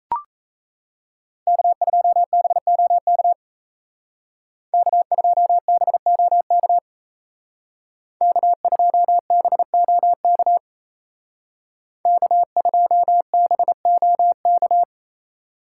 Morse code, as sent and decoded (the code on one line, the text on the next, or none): -.- ..--- -... --- -.- / -.- ..--- -... --- -.- / -.- ..--- -... --- -.- / -.- ..--- -... --- -.-
K2BOK K2BOK K2BOK K2BOK